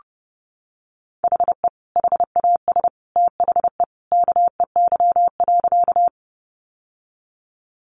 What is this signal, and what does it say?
Morse code, heard as a keyed tone, about 30 wpm, 700 Hz. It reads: HE HAS THE KEY.
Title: Sentences from Top 600 Words 30wpm